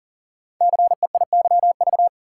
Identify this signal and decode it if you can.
Morse code, heard as a keyed tone, about 42 wpm, 700 Hz. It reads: CEIYV
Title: Code Groups - 5 Letter:Numbers - 40wpm